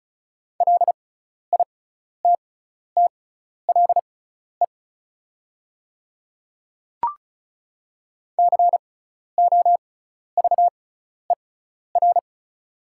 Morse code, heard as a keyed tone, about 36 wpm, spaced as for 12 wpm, 700 Hz. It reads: LITTLE COVER
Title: 6x Character Spacing - Top 300 Words - 35wpm